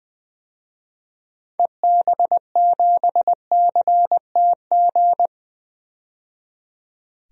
Morse code, heard as a keyed tone, about 21 wpm, 700 Hz. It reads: EB7CTG
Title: DXCC-100 Call Signs - 6 Characters - 20wpm